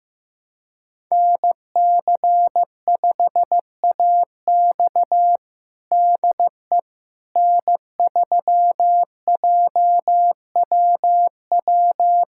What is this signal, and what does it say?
Morse code, heard as a keyed tone, about 15 wpm, 700 Hz. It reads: NC5AX DE N3JWW
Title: USA Callsign DE Callsign - 5 Characters - 15wpm